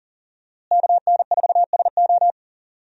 Morse code, heard as a keyed tone, about 37 wpm, 700 Hz. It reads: KN4SO
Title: USA Call Signs - 5 Characters - Repeated 3x - 40wpm